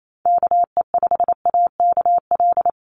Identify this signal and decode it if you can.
Morse code, heard as a keyed tone, about 28 wpm, 700 Hz. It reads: KE5AKL
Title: NA SOTA Activator Callsigns 28wpm